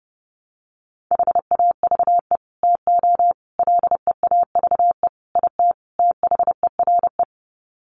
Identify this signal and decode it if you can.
Morse code, heard as a keyed tone, about 30 wpm, 700 Hz. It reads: HAVE TO LEAVE IT THERE
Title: Sets of 5 Words Encore2 30wpm